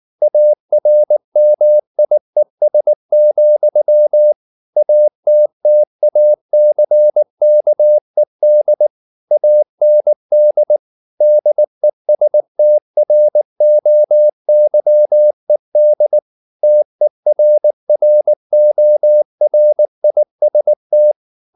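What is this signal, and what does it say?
Morse code, heard as a keyed tone, about 19 wpm, 600 Hz. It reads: ARMIES, ATTACKED AND DESTROYED TERRORIST
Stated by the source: Morse code at 19 wpm from CW med Gnister, reading Ken Follett's novel Never in English.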